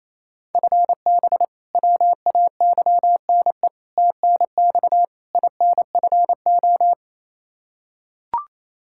Morse code, heard as a keyed tone, about 28 wpm, 700 Hz. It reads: FB WAYNE TNX INFO
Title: Practice QSO - Exchange 3 - More Info - 28wpm